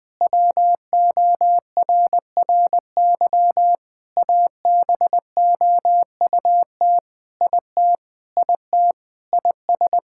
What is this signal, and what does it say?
Morse code, heard as a keyed tone, about 20 wpm, 700 Hz. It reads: WORRY ABOUT IT IT IS
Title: Sentences with 5-Word Phrases - Spoken Beginning - 20wpm